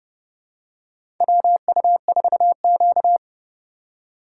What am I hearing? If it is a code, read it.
Morse code, heard as a keyed tone, about 30 wpm, 700 Hz. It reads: WU4Q
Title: USA Call Signs - 4 Characters - 3x Character Spacing - 30wpm